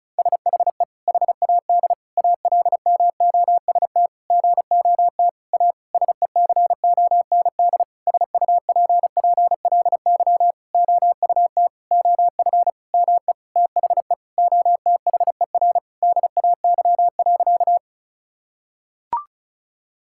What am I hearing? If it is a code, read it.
Morse code, heard as a keyed tone, about 36 wpm, 700 Hz. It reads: SHE HAD ALMOST GOT A SECOND SUPPLY OUT OF ME THE OTHER DAY.
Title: Sentences from Top 1000 Words - Review Part 5 - 35wpm